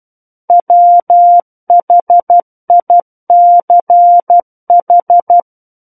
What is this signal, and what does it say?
Morse code, heard as a keyed tone, about 12 wpm, 700 Hz. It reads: WHICH
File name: Book One / 03-Chapter 3 - 12 WPM